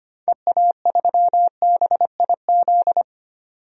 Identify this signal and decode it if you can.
Morse code, heard as a keyed tone, about 25 wpm, 700 Hz. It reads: EA3BIZ
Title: DXCC-100 Call Signs - 6 Characters - 25wpm